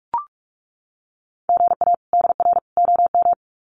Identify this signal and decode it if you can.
Morse code, heard as a keyed tone, about 44 wpm, 700 Hz. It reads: GA DR OM
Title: Common QSO Phrases 45wpm